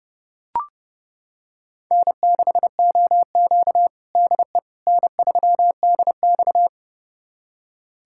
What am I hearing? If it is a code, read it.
Morse code, heard as a keyed tone, about 30 wpm, 700 Hz. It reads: N6OQ DE N3DX
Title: USA Callsign DE Callsign - 4 Characters - 30wpm